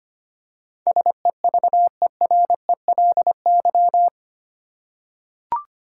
Morse code, heard as a keyed tone, about 25 wpm, 700 Hz. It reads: SEVERELY
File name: Eight Letter Words - 25wpm